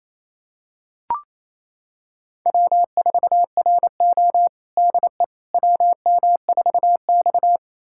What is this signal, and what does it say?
Morse code, heard as a keyed tone, about 27 wpm, 690 Hz. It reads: W4RO DE WM4X
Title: USA Callsign DE Callsign - 4 Characters - 28wpm